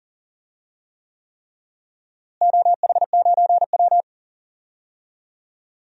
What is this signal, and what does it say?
Morse code, heard as a keyed tone, about 37 wpm, 700 Hz. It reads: OH9W
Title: DXCC-100 Call Signs - 4 Characters - Repeated 3x - 40wpm